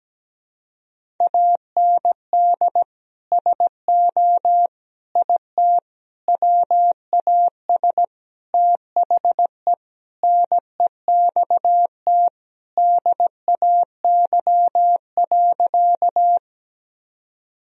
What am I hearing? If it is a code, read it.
Morse code, heard as a keyed tone, about 17 wpm, 700 Hz. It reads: AND SO IT WAS THE NEXT DAY.
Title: Sentences from Top 1000 Words - Review Part 1 - 17wpm